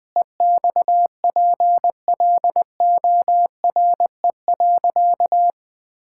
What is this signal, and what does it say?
Morse code, heard as a keyed tone, about 20 wpm, 700 Hz. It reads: EXPLORE.
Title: Sentences with 2-Word Phrases - 20wpm